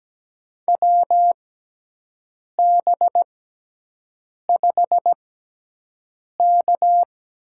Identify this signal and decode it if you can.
Morse code, heard as a keyed tone, about 17 wpm, spaced as for 6 wpm, 700 Hz. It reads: WB5K
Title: USA Call Signs - 6 Characters - 6x Character Spacing - 17wpm